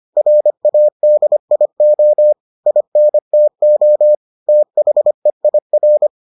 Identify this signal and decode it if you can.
Morse code, heard as a keyed tone, about 25 wpm, 600 Hz. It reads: RADIO INTO THEIR